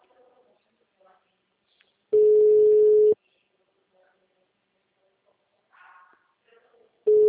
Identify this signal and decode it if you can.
Ringback tone (Europe, 425 Hz).
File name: out- -2014- - - .88735